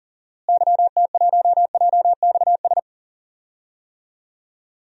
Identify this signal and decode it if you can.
Morse code, heard as a keyed tone, about 40 wpm, 700 Hz. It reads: YT1JXS
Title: DXCC-100 Call Signs - 6 Characters - Repeated 2x - 40wpm